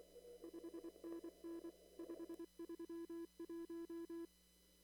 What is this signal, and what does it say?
Morse code, heard as a keyed tone, about 24 wpm, 340 Hz. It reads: 5NN 531
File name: CW / kt5j wpx cw-op-k5pi